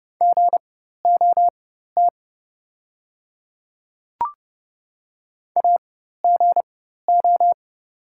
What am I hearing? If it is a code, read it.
Morse code, heard as a keyed tone, about 30 wpm, spaced as for 14 wpm, 700 Hz. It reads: GOT AGO
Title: 4x Character Spacing - Top 500 Words - 30wpm